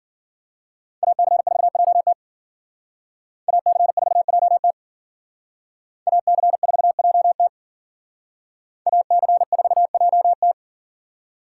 Morse code, heard as a keyed tone, about 51 wpm, 700 Hz. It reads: AC4JT AC4JT AC4JT AC4JT